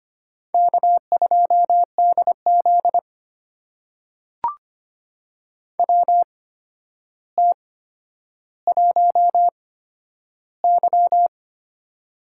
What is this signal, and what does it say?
Morse code, heard as a keyed tone, about 25 wpm, spaced as for 11 wpm, 700 Hz. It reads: K2DZ WT1Y
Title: USA Call Signs - 8x Character Spacing - 25wpm